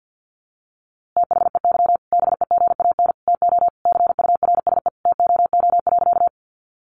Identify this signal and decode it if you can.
Morse code, heard as a keyed tone, about 48 wpm, 700 Hz. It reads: THEY BEGAN TO CURSE TOO.
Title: A Sound Of Thunder 50wpm